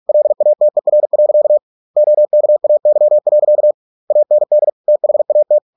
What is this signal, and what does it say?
Morse code, heard as a keyed tone, about 46 wpm, 600 Hz. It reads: LATER. OKAY. AND THAT